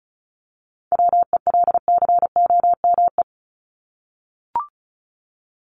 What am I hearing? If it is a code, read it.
Morse code, heard as a keyed tone, about 34 wpm, 700 Hz. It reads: WELCOME
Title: Top 1751-2000 Words 35wpm